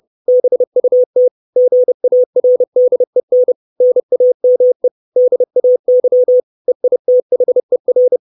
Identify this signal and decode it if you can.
Morse code, heard as a keyed tone, about 30 wpm, 500 Hz. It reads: BUT GARDEN NAME DAY EITHER
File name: 632 Words 30WPM